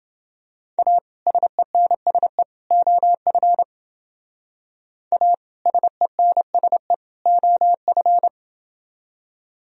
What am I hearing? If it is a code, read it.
Morse code, heard as a keyed tone, about 28 wpm, 700 Hz. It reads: A SENSE OF A SENSE OF